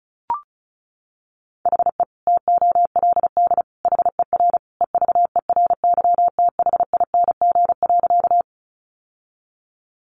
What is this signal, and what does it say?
Morse code, heard as a keyed tone, about 34 wpm, 700 Hz. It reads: HE TOLD HER EVERYTHING.
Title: Sentences from Top 700 Words 35wpm